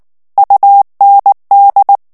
Morse code, heard as a keyed tone, about 19 wpm, 800 Hz. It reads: UND
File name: morse mp3 / ALLEMAND texte clair 2 vitesse 18 mots minute